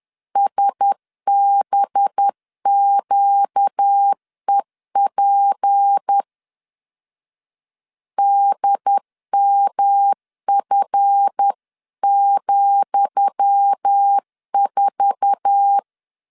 Morse code, heard as a keyed tone, about 10 wpm, 800 Hz. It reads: SBQEP DMF,4
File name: LEZIONI / 10 SIGNS / 10D COMA 10wpm 3013